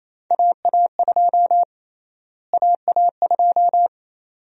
Morse code, heard as a keyed tone, about 28 wpm, 700 Hz. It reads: AA2 AA2
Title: USA Call Sign Prefixes - Repeated 2x - 28wpm